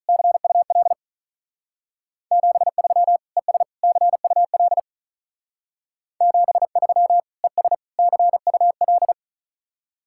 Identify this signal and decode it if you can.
Morse code, heard as a keyed tone, about 42 wpm, 700 Hz. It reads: CUL 73 ES CUL 73 ES CUL
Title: Speed-Racing Common QSO Phrases - 35wpm